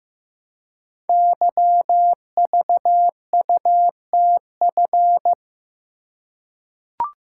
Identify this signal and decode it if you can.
Morse code, heard as a keyed tone, about 15 wpm, 700 Hz. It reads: YVUTF
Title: Code Groups - 5 Letter:Numbers - 15wpm